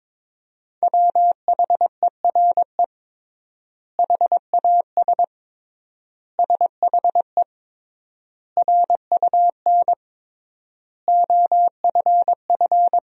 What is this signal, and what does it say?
Morse code, heard as a keyed tone, about 22 wpm, 700 Hz. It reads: WHERE HAS SHE RUN OFF
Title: 3x Word-Spacing - Sentences from Top 300 Words - 22wpm